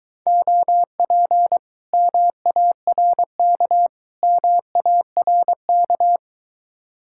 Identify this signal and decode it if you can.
Morse code, heard as a keyed tone, about 24 wpm, 700 Hz. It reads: OP MARK MARK